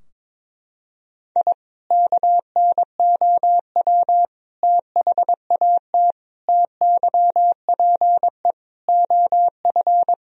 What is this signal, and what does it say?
Morse code, heard as a keyed tone, about 22 wpm, 700 Hz. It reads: I KNOW THAT TYPE OF
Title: Sentences from Top 1000 Words - Review Part 3 - 22wpm